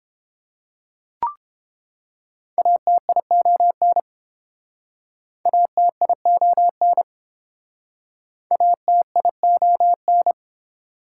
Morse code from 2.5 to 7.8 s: ATION ATION

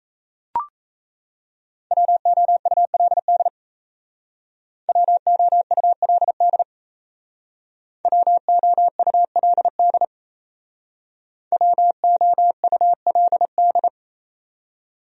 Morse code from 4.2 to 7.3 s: WOULD

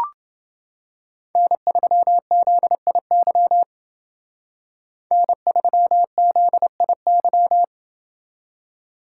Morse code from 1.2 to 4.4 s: N3ZIY

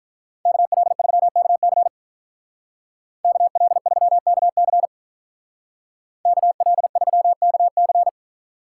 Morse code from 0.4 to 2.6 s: KL3KC

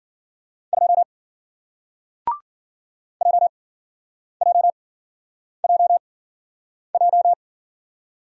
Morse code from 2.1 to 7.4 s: JJJJ

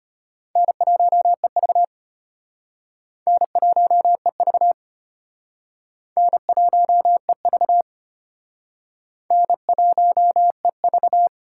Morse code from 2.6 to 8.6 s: N1EV N1EV

